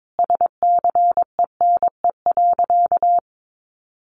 Morse code, heard as a keyed tone, about 22 wpm, 700 Hz. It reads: SCENE.
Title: Sentences from Top 1100 Words 22wpm